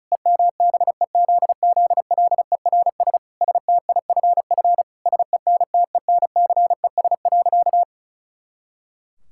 Morse code, heard as a keyed tone, about 36 wpm, 700 Hz. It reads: EMBEZZLERS STIFF SENTENCES.
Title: Pangram Challenge v2 - 35wpm